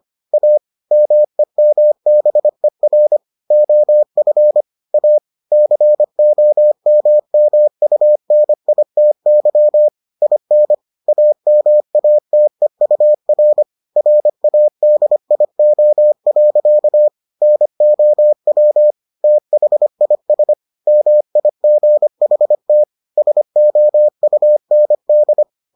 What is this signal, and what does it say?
Morse code, heard as a keyed tone, about 25 wpm, 600 Hz. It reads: A MEMBER OF A COMMUNITY IN AMATEUR RADIO. NOW THIS MIGHT SOUND